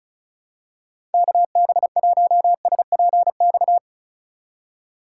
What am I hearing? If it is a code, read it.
Morse code, heard as a keyed tone, about 36 wpm, 700 Hz. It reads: KB1SPX
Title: USA Call Signs - 6 Characters - 2x Character Spacing - 35wpm